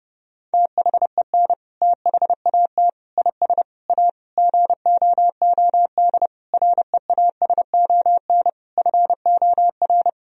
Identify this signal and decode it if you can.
Morse code, heard as a keyed tone, about 30 wpm, 700 Hz. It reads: THEN THAT IS A GOOD REASON FOR
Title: Sentences from Top 1000 Words - Review Part 4 - 30wpm